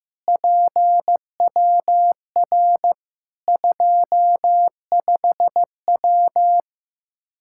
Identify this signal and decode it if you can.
Morse code, heard as a keyed tone, about 15 wpm, 700 Hz. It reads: PWR 25W
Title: Practice QSO - Exchange 2 - Info - 15wpm